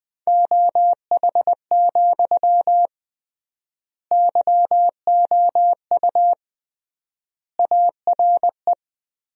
Morse code, heard as a keyed tone, about 20 wpm, 700 Hz. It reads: OH, YOU ARE
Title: 3x Word-Spacing - Sentences from Top 500 Words - 20wpm